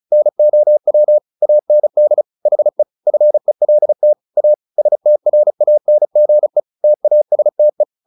Morse code, heard as a keyed tone, about 34 wpm, 600 Hz. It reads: NOW AND HE FELT A STRANGE TASTE
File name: the old man and the sea 2 35wpm